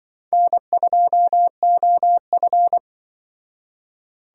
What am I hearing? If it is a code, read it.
Morse code, heard as a keyed tone, about 25 wpm, 700 Hz. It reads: N2OF